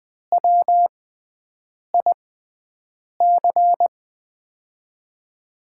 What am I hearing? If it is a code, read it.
Morse code, heard as a keyed tone, about 20 wpm, spaced as for 7 wpm, 700 Hz. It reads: WIC
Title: USA Call Signs - 5 Characters - 6x Character Spacing - 20wpm